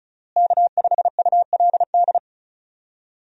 Morse code, heard as a keyed tone, about 34 wpm, 700 Hz. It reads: K5ULD
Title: USA Call Signs - 5 Characters - Repeated 2x - 35wpm